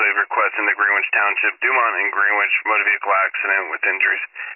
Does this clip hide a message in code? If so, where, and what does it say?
Yes: Emergency Alert System (SAME) tones from 0.6 to 3.7 s, heard as harsh buzzing bursts, with speech.